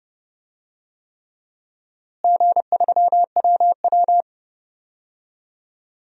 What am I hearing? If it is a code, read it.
Morse code, heard as a keyed tone, about 30 wpm, 700 Hz. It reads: G3WW
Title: DXCC-100 Call Signs - 4 Characters - Repeated 3x - 30wpm